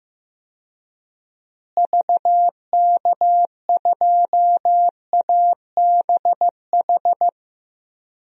Morse code, heard as a keyed tone, about 16 wpm, 700 Hz. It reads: VK2ABH
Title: DXCC-100 Call Signs - 6 Characters - Repeated 2x - 15wpm